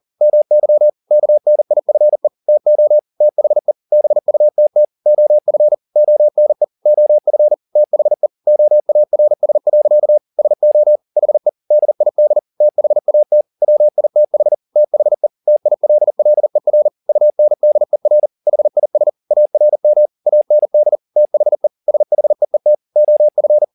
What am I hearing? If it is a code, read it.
Morse code, heard as a keyed tone, about 40 wpm, 600 Hz. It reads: MY KNIFE TO THE BUTT OF ONE OF THE OARS. SO HE DID THAT WITH THE TILLER UNDER HIS ARM AND THE SHEET OF
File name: the old man and the sea 2 40wpm